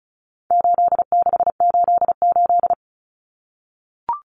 Morse code, heard as a keyed tone, about 34 wpm, 700 Hz. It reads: 8688
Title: Numbers - 4-Digit 35wpm